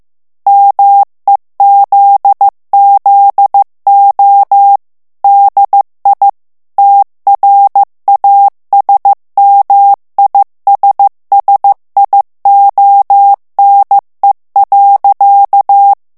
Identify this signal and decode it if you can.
Morse code, heard as a keyed tone, about 15 wpm, 800 Hz. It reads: MEZZO DI TRASMISSIONE.
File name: ITALIEN texte clair 1 vitesse 14 mots minute